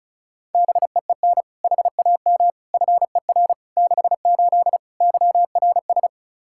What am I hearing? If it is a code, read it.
Morse code, heard as a keyed tone, about 36 wpm, 700 Hz. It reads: BEEN HAM FER 68 YRS
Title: Practice QSO - Exchange 3 - More Info - 35wpm